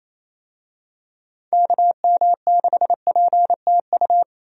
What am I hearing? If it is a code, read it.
Morse code, heard as a keyed tone, about 28 wpm, 700 Hz. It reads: KM6PTU
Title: USA Call Signs - 6 Characters - 28wpm